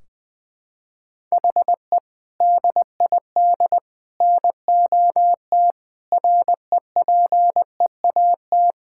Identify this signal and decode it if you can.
Morse code, heard as a keyed tone, about 20 wpm, 700 Hz. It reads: HE DID NOT REPEAT
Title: Sentences from Top 1600 Words 20wpm